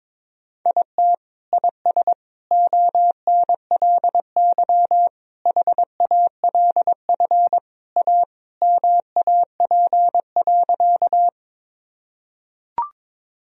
Morse code, heard as a keyed tone, about 22 wpm, 700 Hz. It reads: IT IS ONLY HALF A MAP.
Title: Sentences from Top 1000 Words - Review Part 2 - 22wpm